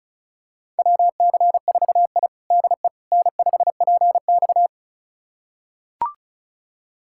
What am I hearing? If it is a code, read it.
Morse code, heard as a keyed tone, about 36 wpm, 700 Hz. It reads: WC4I DE N5PX
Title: USA Callsign DE Callsign - 4 Characters - 35wpm